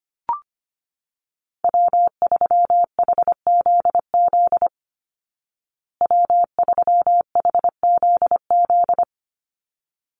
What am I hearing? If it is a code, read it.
Morse code, heard as a keyed tone, about 25 wpm, 700 Hz. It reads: W3HZZ W3HZZ
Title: Call Signs Repeated 3x - 25wpm